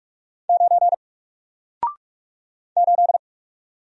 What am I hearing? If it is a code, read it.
Morse code, heard as a keyed tone, about 43 wpm, 700 Hz. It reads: 9 8
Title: Single Letter-Number Alt 45wpm